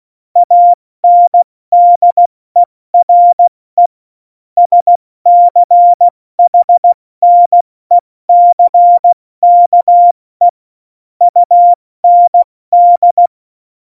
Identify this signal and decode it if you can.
Morse code, heard as a keyed tone, about 16 wpm, 700 Hz. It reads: ANDERE SCHNECKE UND